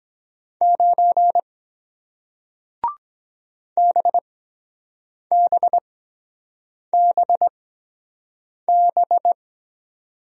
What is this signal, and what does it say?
Morse code, heard as a keyed tone, about 23 wpm, spaced as for 7 wpm, 700 Hz. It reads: 9 BBBB